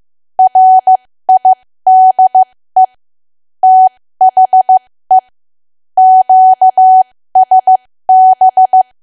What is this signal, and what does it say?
Morse code, heard as a keyed tone, about 15 wpm, spaced as for 13 wpm, 750 Hz. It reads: RIDE THE QSB
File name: cwa / W1AW-Level2 / 13WPM